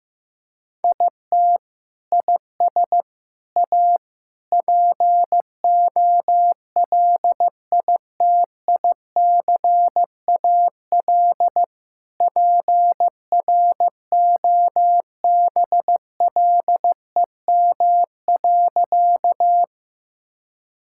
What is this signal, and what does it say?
Morse code, heard as a keyed tone, about 16 wpm, 700 Hz. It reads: IT IS A POLITICAL PROBLEM.
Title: Sentences from Top 1000 Words - Review Part 5 - 15wpm